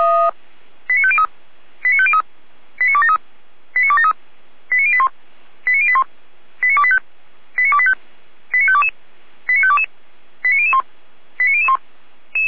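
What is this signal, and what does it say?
Five-tone selective call (ZVEI), heard as quick runs of five tones, tones of about 70 ms. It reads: 89503 89503 89273 89273 89081 89081 89285 89285 89522 89522 89001 89001
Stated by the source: ZVEI five-tone call received on SDR, from the audio archive of the Orlamünde fire brigade (feuerwehr-orlamuende).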